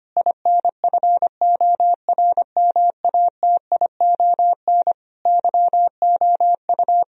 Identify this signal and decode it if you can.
Morse code, heard as a keyed tone, about 25 wpm, 700 Hz. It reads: INFORMATION YOU
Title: Sentences from Top 1000 Words - Review Part 3 - 25wpm